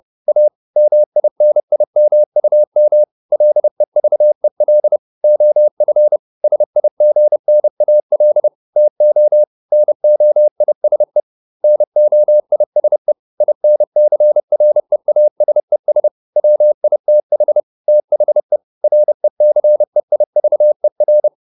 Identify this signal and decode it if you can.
Morse code, heard as a keyed tone, about 30 wpm, 600 Hz. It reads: A MINIMUM LEVEL OF SIGNAL TO NOISE NOISE INCREASES WITH THE RECEIVER